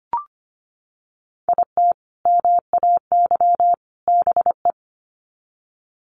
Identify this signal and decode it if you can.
Morse code, heard as a keyed tone, about 25 wpm, 700 Hz. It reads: IT MAY BE
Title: Sets of 3 Words 25wpm